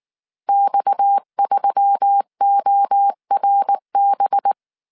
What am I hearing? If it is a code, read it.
Morse code, heard as a keyed tone, about 19 wpm, 800 Hz. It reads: X3ORB
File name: CW / DET / 1 6 1500C S C 18wpm 2356